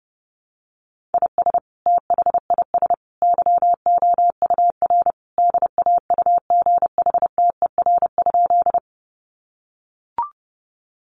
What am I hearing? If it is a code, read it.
Morse code, heard as a keyed tone, about 30 wpm, 700 Hz. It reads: IS THIS YOUR DAUGHTER?
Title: Sentences from Top 1500 Words 30wpm